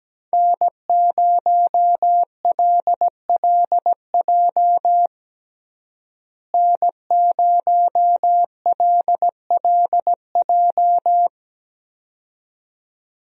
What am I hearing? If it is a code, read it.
Morse code, heard as a keyed tone, about 17 wpm, 700 Hz. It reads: N0LLJ N0LLJ
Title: USA Call Signs - 5 Characters - Repeated 2x - 17wpm